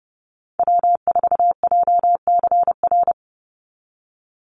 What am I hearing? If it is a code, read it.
Morse code, heard as a keyed tone, about 30 wpm, 700 Hz. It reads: W4JCR
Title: USA Call Signs 30wpm